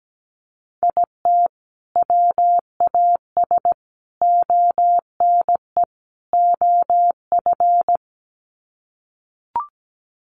Morse code, heard as a keyed tone, about 17 wpm, 700 Hz. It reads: IT WAS ONE OF
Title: Sets of 4 Words 17wpm